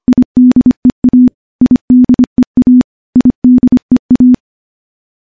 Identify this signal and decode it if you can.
Morse code, heard as a keyed tone, about 25 wpm, 250 Hz. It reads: IDEA IDEA IDEA
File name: morse-training-25WPM